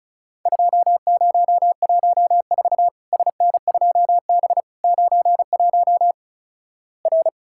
Morse code, heard as a keyed tone, about 36 wpm, 700 Hz. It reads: 2014 SN2B 91 R
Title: Worked All Europe DX Contest - QTC Exchange - 35wpm